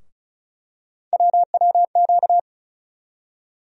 Morse code, heard as a keyed tone, about 36 wpm, 700 Hz. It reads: WWQ
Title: USA Call Sign Postfixes - 3 Letters 35wpm